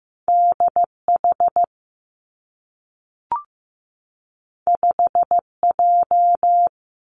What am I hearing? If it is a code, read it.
Morse code, heard as a keyed tone, about 15 wpm, 700 Hz. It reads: DH 5J
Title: Two Challenging Letter-Number Combinations 15wpm